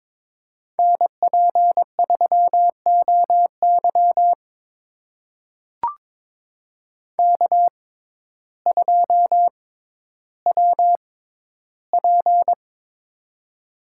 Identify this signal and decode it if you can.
Morse code, heard as a keyed tone, about 22 wpm, spaced as for 12 wpm, 700 Hz. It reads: NP3OY K2WP
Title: USA Call Signs - 6x Character Spacing - 22wpm